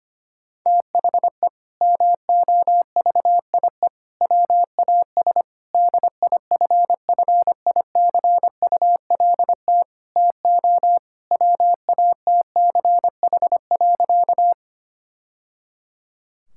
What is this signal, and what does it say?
Morse code, heard as a keyed tone, about 25 wpm, 700 Hz. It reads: THE MOVIE WAS DIFFICULT TO WATCH.
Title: Sentences from Top 1000 Words - Review Part 5 - 25wpm